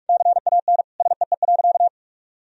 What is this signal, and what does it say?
Morse code, heard as a keyed tone, about 46 wpm, 700 Hz. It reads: CAN SEE.
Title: Sentences from Top 1000 Words - Review Part 1 - 45wpm